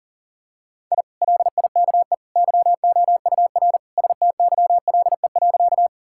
Morse code, heard as a keyed tone, about 37 wpm, 700 Hz. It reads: I LIKE YOUR STYLE.
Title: Sentences from Top 1000 Words - Review Part 5 - 40wpm